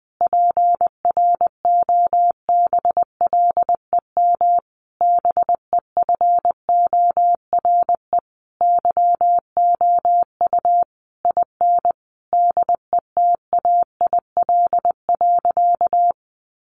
Morse code, heard as a keyed tone, about 20 wpm, 700 Hz. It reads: PROBLEM BEFORE YOU IN DETAIL.
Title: Sentences from Top 1500 Words 20wpm